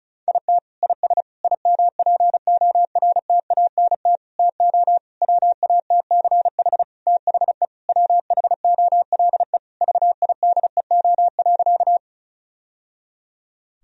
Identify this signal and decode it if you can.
Morse code, heard as a keyed tone, about 36 wpm, 700 Hz. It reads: IT IS IMPORTANT TO WATCH THE WHOLE VIDEO.
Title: Sentences from Top 1000 Words - Review Part 4 - 35wpm